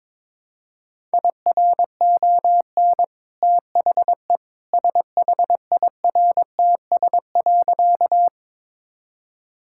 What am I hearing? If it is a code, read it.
Morse code, heard as a keyed tone, about 22 wpm, 690 Hz. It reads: IRON THE SHIRTS.
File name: practice / 22wpm / Easy Sentences with Hints - 22wpm